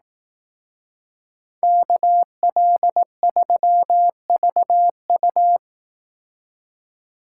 Morse code, heard as a keyed tone, about 17 wpm, 700 Hz. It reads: KL3VU
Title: Speed Racing - USA 2x2 Call Signs 15wpm